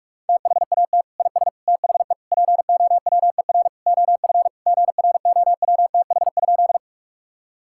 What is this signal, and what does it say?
Morse code, heard as a keyed tone, about 41 wpm, 700 Hz. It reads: THAT IS THE POWER OF GROWTH?
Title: Sentences from Top 1000 Words - Review Part 5 - 45wpm